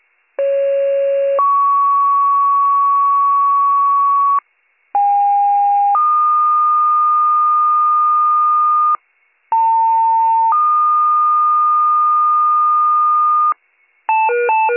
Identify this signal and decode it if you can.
Two-tone page, heard as a one-second tone, then a three-second tone, about 550 then 1120 Hz, about 810 then 1220 Hz, about 900 then 1220 Hz. It is heard from 0.4 to 13.5 s.